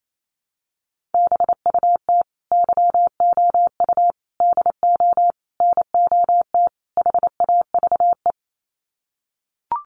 Morse code, heard as a keyed tone, about 28 wpm, 700 Hz. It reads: BUT YOU DO NOT HAVE
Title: Sets of 5 Words Encore 28wpm